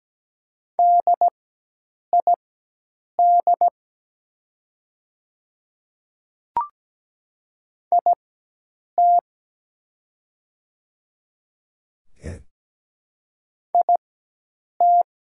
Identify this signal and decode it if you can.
Morse code, heard as a keyed tone, about 17 wpm, spaced as for 8 wpm, 700 Hz. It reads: DID IT IT
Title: 4x Character Spacing - Top 200 Words - 17wpm